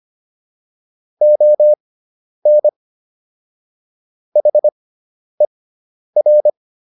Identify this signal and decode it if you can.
Morse code, heard as a keyed tone, about 25 wpm, spaced as for 10 wpm, 600 Hz. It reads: ON HER